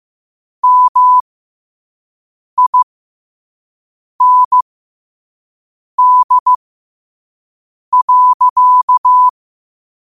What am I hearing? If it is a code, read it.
Morse code, heard as a keyed tone, about 15 wpm, spaced as for 5 wpm, 1000 Hz. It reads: MIND.
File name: qotdfiles / 5wpm